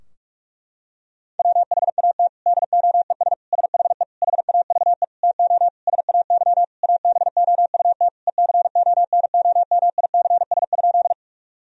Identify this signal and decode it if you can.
Morse code, heard as a keyed tone, about 46 wpm, 700 Hz. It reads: WHAT DOES SHE HAVE TO SAY ABOUT ECONOMICS?
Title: Sentences from Top 1000 Words - Review Part 3 - 45wpm